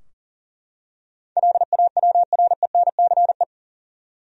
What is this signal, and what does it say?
Morse code, heard as a keyed tone, about 40 wpm, 700 Hz. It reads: LAWRENCE
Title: USA Common Names - 40wpm